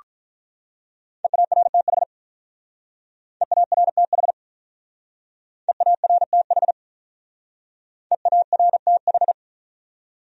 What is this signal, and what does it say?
Morse code, heard as a keyed tone, about 41 wpm, 700 Hz. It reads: EARTH EARTH EARTH EARTH
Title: Speed-Racing Top 300 Words 35wpm